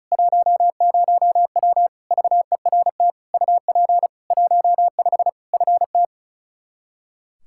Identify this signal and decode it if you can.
Morse code, heard as a keyed tone, about 36 wpm, 700 Hz. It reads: 10W VERT UP 15 FT
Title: Practice QSO - Exchange 2 - Info - 35wpm